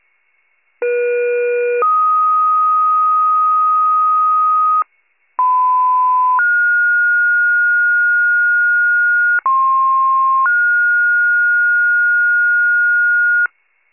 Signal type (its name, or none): two-tone page